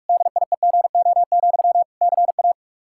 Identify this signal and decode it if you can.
Morse code, heard as a keyed tone, about 41 wpm, 700 Hz. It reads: DIEGO, CA
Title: Practice QSO - Exchange 1 - Beginning - 45wpm